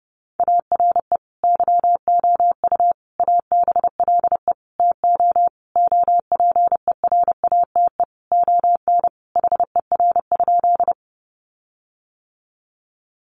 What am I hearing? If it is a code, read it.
Morse code, heard as a keyed tone, about 30 wpm, 700 Hz. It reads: ARE YOU ABLE TO OPERATE ON HER?
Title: Sentences from Top 1500 Words 30wpm